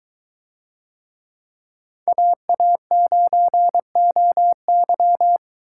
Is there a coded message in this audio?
Yes: Morse code from 2.1 to 5.4 s, heard as a keyed tone, about 22 wpm, 700 Hz. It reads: AA9OY